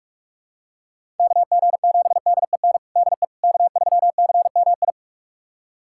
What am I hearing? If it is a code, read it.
Morse code, heard as a keyed tone, about 46 wpm, 700 Hz. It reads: KG7DEN DE K3CMI
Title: USA Callsign DE Callsign 45wpm